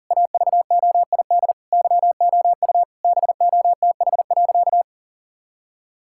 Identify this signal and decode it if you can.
Morse code, heard as a keyed tone, about 40 wpm, 700 Hz. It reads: AVOID YOU BOTH.
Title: Sentences from Top 1000 Words - Review Part 6 - 40wpm